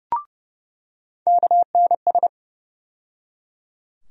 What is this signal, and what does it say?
Morse code, heard as a keyed tone, about 30 wpm, 700 Hz. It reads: KNS
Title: USA Call Sign Postfixes - 3 Letters 30wpm